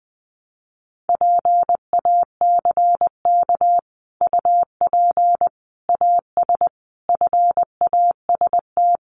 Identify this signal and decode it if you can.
Morse code, heard as a keyed tone, about 20 wpm, 700 Hz. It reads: PACK UP AS FAST
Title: Sentences from Top 1300 Words 20wpm